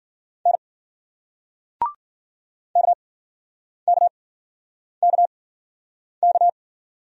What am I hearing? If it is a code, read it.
Morse code, heard as a keyed tone, about 47 wpm, spaced as for 9 wpm, 700 Hz. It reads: N KKKK
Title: Speed-Racing Single Letters 40wpm